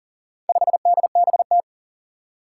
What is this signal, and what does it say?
Morse code, heard as a keyed tone, about 40 wpm, 700 Hz. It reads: 5DBT
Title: Four Challenging Letter-Number Combinations 40wpm